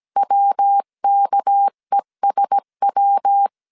Morse code, heard as a keyed tone, about 17 wpm, 800 Hz. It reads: WKESW
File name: CW / DET / 1 6 1500C S C 16wpm 2733